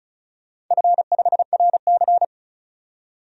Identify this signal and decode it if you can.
Morse code, heard as a keyed tone, about 36 wpm, 700 Hz. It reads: F5RC
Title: DXCC-100 Call Signs - 4 Characters - 35wpm